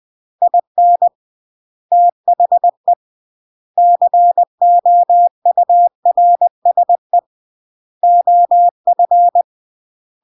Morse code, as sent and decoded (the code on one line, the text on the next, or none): .. -. / - .... / -.-. --- ..- .-. ... / --- ..-.
IN THE COURSE OF